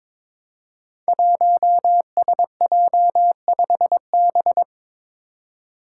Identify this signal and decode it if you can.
Morse code, heard as a keyed tone, about 22 wpm, 700 Hz. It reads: 1SJ5B
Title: Five Challenging Letter-Number Combinations 22wpm